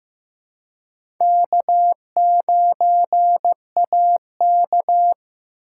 Morse code, heard as a keyed tone, about 15 wpm, 700 Hz. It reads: K9AK
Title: USA Call Signs - 4 Characters - 3x Character Spacing - 15wpm